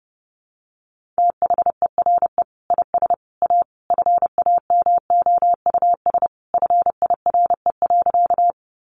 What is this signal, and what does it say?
Morse code, heard as a keyed tone, about 30 wpm, 700 Hz. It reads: THERE IS A FAMOUS FIRE.
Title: Sentences from Top 1500 Words 30wpm